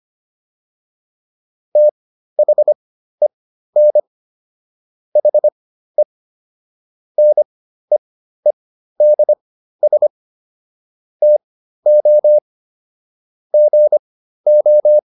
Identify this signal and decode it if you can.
Morse code, heard as a keyed tone, about 25 wpm, spaced as for 13 wpm, 600 Hz. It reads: THEN HE NEEDS TO GO